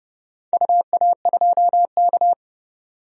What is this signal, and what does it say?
Morse code, heard as a keyed tone, about 30 wpm, 700 Hz. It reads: UA2K
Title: DXCC-100 Call Signs - 4 Characters - 30wpm